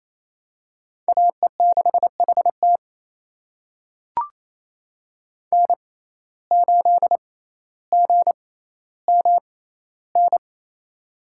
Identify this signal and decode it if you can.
Morse code, heard as a keyed tone, about 27 wpm, spaced as for 14 wpm, 700 Hz. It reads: AE6HT N8GMN
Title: USA Call Signs - 5 Characters - 6x Character Spacing - 28wpm